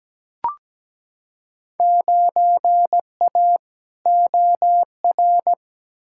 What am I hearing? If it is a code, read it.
Morse code, heard as a keyed tone, about 17 wpm, 700 Hz. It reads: 9A OR
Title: Field Day Exchanges Encore - 17wpm